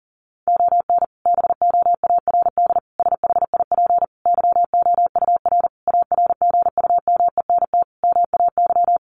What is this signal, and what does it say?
Morse code, heard as a keyed tone, about 37 wpm, 700 Hz. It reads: ON BOARD SHIP YOUR ARGUMENT MAY
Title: Sentences from Top 1400 Words 40wpm